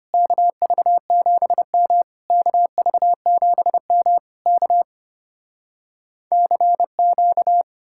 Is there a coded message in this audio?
Yes: Morse code from 0.1 to 7.6 s, heard as a keyed tone, about 33 wpm, 700 Hz. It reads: KV7M KV7M K CQ